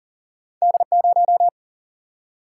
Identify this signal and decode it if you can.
Morse code, heard as a keyed tone, about 40 wpm, 700 Hz. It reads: D0
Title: Single-Letter Single-Number 40wpm